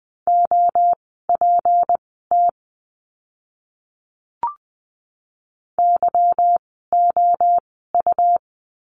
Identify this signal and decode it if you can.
Morse code, heard as a keyed tone, about 20 wpm, spaced as for 15 wpm, 700 Hz. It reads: OPT YOU
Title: 153 Lesson 25f - Letter p - Words 20wpm